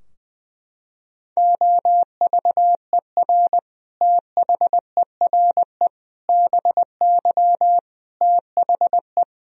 Morse code, heard as a keyed tone, about 20 wpm, 700 Hz. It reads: OVER THERE BY THE
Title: Sentences from Top 1000 Words - Review Part 3 - 20wpm